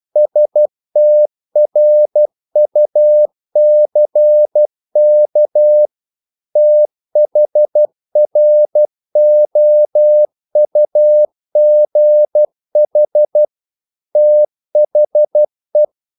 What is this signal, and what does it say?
Morse code, heard as a keyed tone, about 12 wpm, 600 Hz. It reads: STRUCK THROUGH THE
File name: mp3 files / the gardener - 12wpm